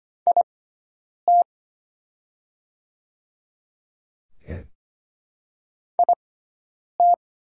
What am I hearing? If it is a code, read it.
Morse code, heard as a keyed tone, about 25 wpm, spaced as for 9 wpm, 700 Hz. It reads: IT IT
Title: 6x Character Spacing - Top 500 Words - 25wpm